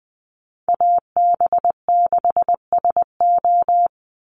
Morse code, heard as a keyed tone, about 20 wpm, 700 Hz. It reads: AB6SO
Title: NA SOTA Chaser Callsigns 20wpm